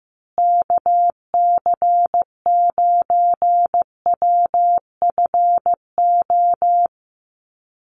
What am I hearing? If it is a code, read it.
Morse code, heard as a keyed tone, about 15 wpm, 700 Hz. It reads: KC9WFO
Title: USA Call Signs Repeated 3x 15wpm